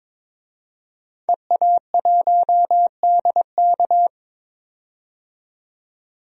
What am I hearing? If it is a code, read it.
Morse code, heard as a keyed tone, about 22 wpm, 700 Hz. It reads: EA1DK
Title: DXCC-100 Call Signs - 5 Characters - 22wpm